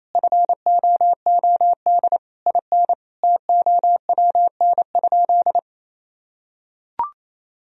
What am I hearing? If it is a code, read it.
Morse code, heard as a keyed tone, about 28 wpm, 700 Hz. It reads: FOOD IN TOWN?
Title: Sentences from Top 1000 Words - Review Part 4 - 28wpm